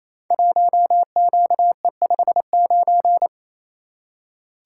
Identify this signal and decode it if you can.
Morse code, heard as a keyed tone, about 29 wpm, 700 Hz. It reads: 1QE59
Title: Code Groups - 5 Letter:Numbers - 28wpm